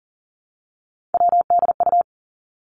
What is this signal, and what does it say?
Morse code, heard as a keyed tone, about 41 wpm, 700 Hz. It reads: WDU
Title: Three Challenging Letter-Number Combinations 40wpm